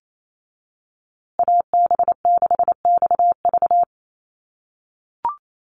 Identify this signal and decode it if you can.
Morse code, heard as a keyed tone, about 28 wpm, 700 Hz. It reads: AB6XV
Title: USA Call Signs Repeated 2x 28wpm